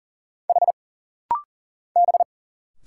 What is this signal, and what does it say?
Morse code, heard as a keyed tone, about 40 wpm, 700 Hz. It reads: H B
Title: Single Letter-Number Alt - Rapid-Fire 40wpm